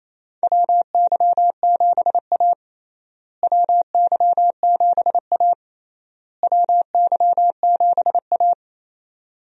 Morse code, heard as a keyed tone, about 28 wpm, 700 Hz. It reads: WY7A WY7A WY7A
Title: USA Call Signs - 4 Characters - Repeated 3x - 28wpm